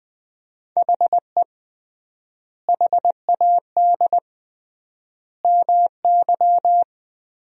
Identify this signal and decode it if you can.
Morse code, heard as a keyed tone, about 20 wpm, 700 Hz. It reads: HE HAD MY
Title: 3x Word-Spacing - Sentences from Top 400 Words - 20wpm